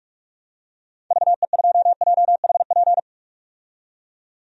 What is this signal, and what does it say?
Morse code, heard as a keyed tone, about 47 wpm, 700 Hz. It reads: VE2JHP
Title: DXCC-100 Call Signs - 6 Characters - 45wpm